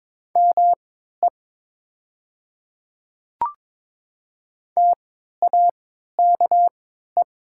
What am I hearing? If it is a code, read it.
Morse code, heard as a keyed tone, about 22 wpm, spaced as for 12 wpm, 700 Hz. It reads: ME TAKE
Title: 3x Character Spacing - Top 500 Words - 22wpm